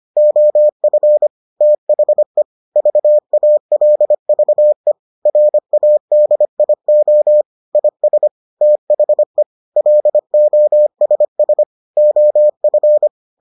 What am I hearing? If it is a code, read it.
Morse code, heard as a keyed tone, about 25 wpm, 600 Hz. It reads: OF THE VALVE RADIO IS THE LOSS OF